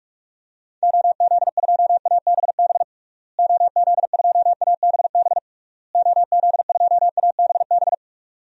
Morse code, heard as a keyed tone, about 46 wpm, 700 Hz. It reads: OZ2ABB OZ2ABB OZ2ABB
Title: DXCC-100 Call Signs - 6 Characters - Repeated 3x - 45wpm